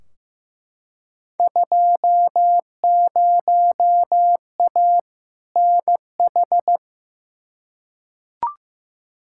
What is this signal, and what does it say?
Morse code, heard as a keyed tone, about 15 wpm, 700 Hz. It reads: 20A NH
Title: Field Day Exchanges Encore - 15wpm